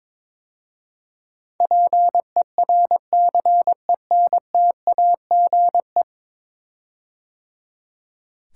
Morse code, heard as a keyed tone, about 22 wpm, 700 Hz. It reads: PERCENTAGE
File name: practice / 22wpm / 10 Letter Words - First 3 Letters Given - 22wpm